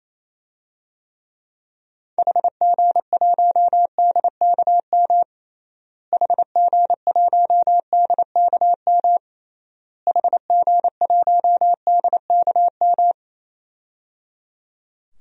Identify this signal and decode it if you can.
Morse code, heard as a keyed tone, about 28 wpm, 700 Hz. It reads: HG1DKM HG1DKM HG1DKM
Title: DXCC-100 Call Signs - 6 Characters - Repeated 3x - 28wpm